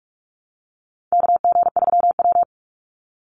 Morse code, heard as a keyed tone, about 44 wpm, 700 Hz. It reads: KG3W
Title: NA SOTA Chaser Callsigns 45wpm